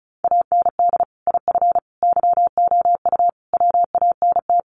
Morse code, heard as a keyed tone, about 34 wpm, 700 Hz. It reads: AND IF YOU WANT
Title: Sets of 5 Words Encore 35wpm